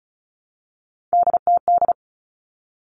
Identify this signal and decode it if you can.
Morse code, heard as a keyed tone, about 36 wpm, 700 Hz. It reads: DTD
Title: Three Challenging Letter-Number Combinations 35wpm